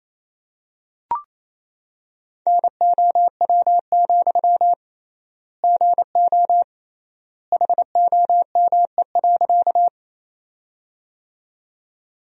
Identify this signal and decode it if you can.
Morse code, heard as a keyed tone, about 27 wpm, 700 Hz. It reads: NOW, GO HOME.
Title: 3x Word-Spacing - Sentences from Top 200 Words - 28wpm